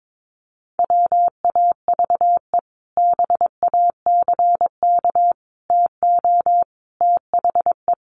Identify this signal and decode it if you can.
Morse code, heard as a keyed tone, about 22 wpm, 700 Hz. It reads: WAVE BACK TO THE
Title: Sentences from Top 1400 Words 22wpm